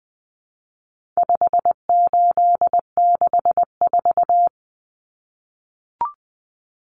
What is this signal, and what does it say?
Morse code, heard as a keyed tone, about 20 wpm, 700 Hz. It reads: 5864
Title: Numbers - 4-Digit 20wpm